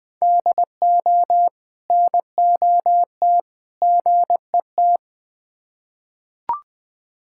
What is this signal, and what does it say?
Morse code, heard as a keyed tone, about 20 wpm, 700 Hz. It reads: DO NOT GET
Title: Build A Phrase - 3-Word Phrases - 20wpm